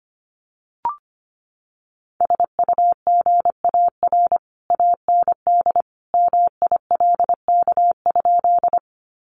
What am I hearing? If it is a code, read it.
Morse code, heard as a keyed tone, about 25 wpm, 700 Hz. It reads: SUGAR AND MILK?
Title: Sentences from Top 1500 Words 25wpm